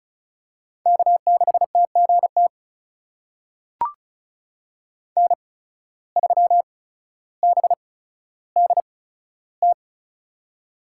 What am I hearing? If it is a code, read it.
Morse code, heard as a keyed tone, about 34 wpm, spaced as for 15 wpm, 700 Hz. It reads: K6TGT N3BDT
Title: USA Call Signs - 5 Characters - 8x Character Spacing - 35wpm